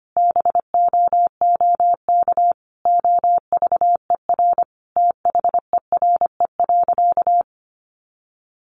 Morse code, heard as a keyed tone, about 25 wpm, 700 Hz. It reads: BOOK OVER THERE.
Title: Sentences from Top 1200 Words 25wpm